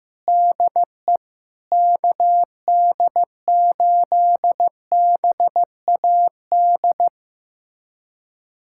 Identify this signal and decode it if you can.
Morse code, heard as a keyed tone, about 16 wpm, 700 Hz. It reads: DE KD8BAD
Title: USA Callsign DE Callsign - 6 Characters - 15wpm